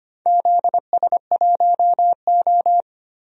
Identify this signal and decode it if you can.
Morse code, heard as a keyed tone, about 25 wpm, 700 Hz. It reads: ZS1O
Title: DXCC-100 Call Signs - 4 Characters - Repeated 3x - 25wpm